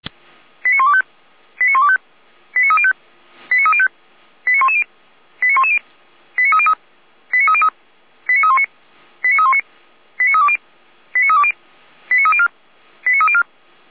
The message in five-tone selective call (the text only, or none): 89125 89125 89305 89305 89110 89110 89303 89303 89319 89319 89320 89320 89304 89304